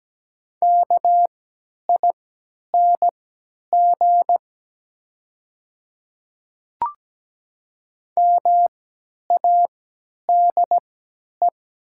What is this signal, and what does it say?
Morse code, heard as a keyed tone, about 17 wpm, spaced as for 10 wpm, 700 Hz. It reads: KING MADE
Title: 3x Character Spacing - Top 500 Words - 17wpm